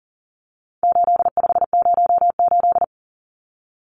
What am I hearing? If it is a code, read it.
Morse code, heard as a keyed tone, about 40 wpm, 700 Hz. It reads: 8508
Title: Numbers - 4-Digit 40wpm